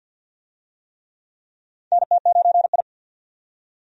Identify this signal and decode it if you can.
Morse code, heard as a keyed tone, about 47 wpm, 700 Hz. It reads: NT9I
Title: USA Call Signs - 4 Characters - 50wpm